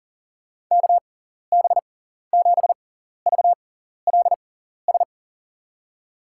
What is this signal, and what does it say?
Morse code, heard as a keyed tone, about 40 wpm, spaced as for 14 wpm, 700 Hz. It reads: KB7VLS
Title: USA Call Signs - 6 Characters - 6x Character Spacing - 40wpm